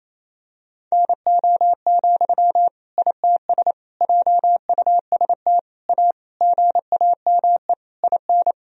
Morse code, heard as a keyed tone, about 28 wpm, 700 Hz. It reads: NO, ITS JUST A GAME IN
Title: Sentences from Top 1000 Words - Review Part 3 - 28wpm